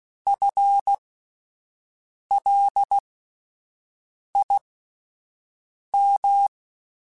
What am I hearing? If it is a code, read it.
Morse code, heard as a keyed tone, about 16 wpm, spaced as for 5 wpm, 800 Hz. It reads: FLIM
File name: Alphabetical / Alphabetical mp3s / Fs @5